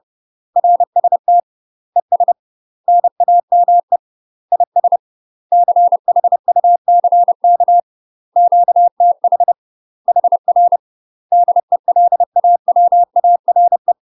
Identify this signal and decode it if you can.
Morse code, heard as a keyed tone, about 30 wpm, 700 Hz. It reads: RST ES NAME IS CHUCK QTH HR DELAWARE
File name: UK MASTER QSOs TRAINING / 30WPM / UK MASTER QSOs 0051